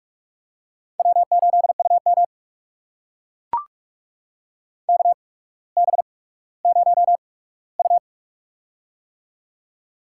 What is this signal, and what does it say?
Morse code, heard as a keyed tone, about 44 wpm, spaced as for 19 wpm, 700 Hz. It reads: W8UM KB0U
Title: USA Call Signs - 4 Characters - 8x Character Spacing - 45wpm